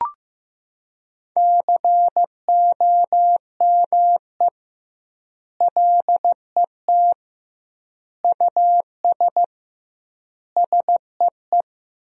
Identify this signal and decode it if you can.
Morse code, heard as a keyed tone, about 16 wpm, 700 Hz. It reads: COME LET US SEE
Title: 2x Word-Spacing - Sentences from Top 300 Words - 15wpm